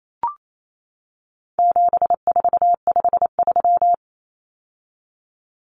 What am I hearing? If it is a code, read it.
Morse code, heard as a keyed tone, about 28 wpm, 700 Hz. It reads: 7453
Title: Numbers - 4-Digit 28wpm